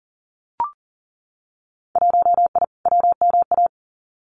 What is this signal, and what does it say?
Morse code, heard as a keyed tone, about 40 wpm, 700 Hz. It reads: 1I WMA
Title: Winter Field Day Exchanges Encore 40wpm